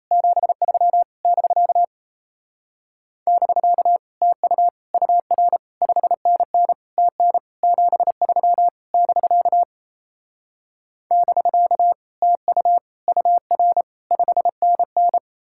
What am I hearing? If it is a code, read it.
Morse code, heard as a keyed tone, about 38 wpm, 700 Hz. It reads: 73 <BK> <BK> TU UR 5NN TN 73 <BK> <BK> TU UR 5NN